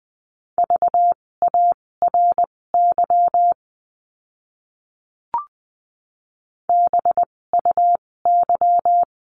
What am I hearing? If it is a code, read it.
Morse code, heard as a keyed tone, about 20 wpm, spaced as for 16 wpm, 700 Hz. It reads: VARY BUY
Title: 256 Lesson 42a - Words 12-20wpm